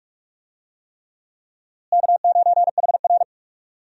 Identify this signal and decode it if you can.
Morse code, heard as a keyed tone, about 42 wpm, 700 Hz. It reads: K9HR
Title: USA Call Signs - 4 Characters - Repeated 3x - 45wpm